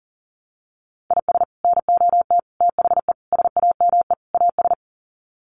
Morse code, heard as a keyed tone, about 40 wpm, 700 Hz. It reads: IS NOT THE SAME AS
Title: Sets of 5 Words Encore 40wpm